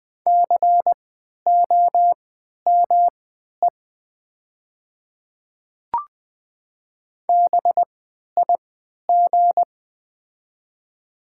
Morse code, heard as a keyed tone, about 20 wpm, spaced as for 11 wpm, 700 Hz. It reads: COME BIG
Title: 3x Character Spacing - Top 500 Words - 20wpm